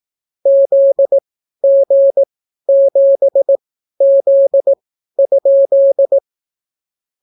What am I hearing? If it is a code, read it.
Morse code, heard as a keyed tone, about 18 wpm, spaced as for 12 wpm, 550 Hz. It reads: ZG7Z?